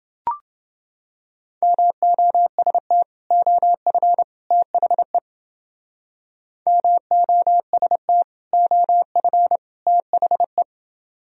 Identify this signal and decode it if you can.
Morse code, heard as a keyed tone, about 29 wpm, 700 Hz. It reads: MOST OF THE MOST OF THE